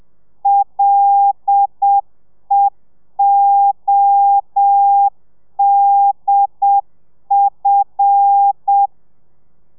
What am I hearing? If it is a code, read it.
Morse code, heard as a keyed tone, about 7 wpm, 800 Hz. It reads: LEODF
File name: telegrafia / i2tfj / Lez5 30